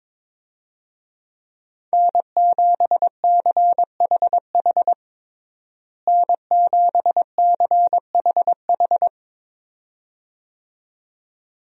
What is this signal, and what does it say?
Morse code, heard as a keyed tone, about 22 wpm, 700 Hz. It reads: N7CHH N7CHH
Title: USA Call Signs - 5 Characters - Repeated 2x - 22wpm